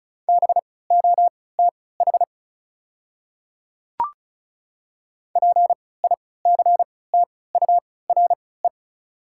Morse code, heard as a keyed tone, about 34 wpm, spaced as for 20 wpm, 700 Hz. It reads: BOTH PICTURE
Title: 3x Character Spacing - Top 300 Words - 35wpm